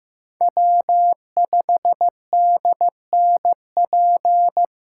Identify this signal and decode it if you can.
Morse code, heard as a keyed tone, about 15 wpm, 700 Hz. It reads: W5DNP
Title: USA Callsign DE Callsign - 5 Characters - 15wpm